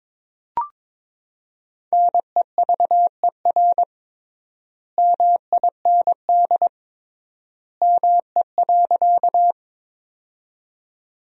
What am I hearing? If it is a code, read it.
Morse code, heard as a keyed tone, about 22 wpm, 700 Hz. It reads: NEVER MIND ME.
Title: 3x Word-Spacing - Sentences from Top 500 Words - 22wpm